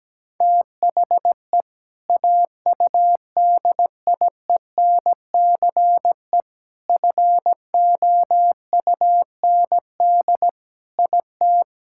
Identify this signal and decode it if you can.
Morse code, heard as a keyed tone, about 17 wpm, 700 Hz. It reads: THE AUDIENCE FOUND IT
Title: Sentences from Top 1000 Words - Review Part 6 - 17wpm